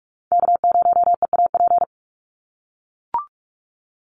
Morse code, heard as a keyed tone, about 44 wpm, 700 Hz. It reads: K0EAP
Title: NA SOTA Chaser Callsigns 45wpm